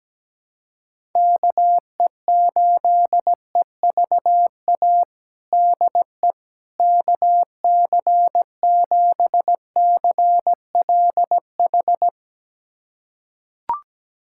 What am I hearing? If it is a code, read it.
Morse code, heard as a keyed tone, about 17 wpm, 700 Hz. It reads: KE8EVA DE KC7CLH
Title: USA Callsign DE Callsign 17wpm